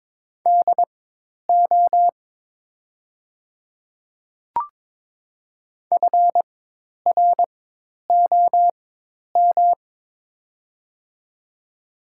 Morse code, heard as a keyed tone, about 22 wpm, spaced as for 10 wpm, 690 Hz. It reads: DO FROM
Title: 4x Character Spacing - Top 300 Words - 22wpm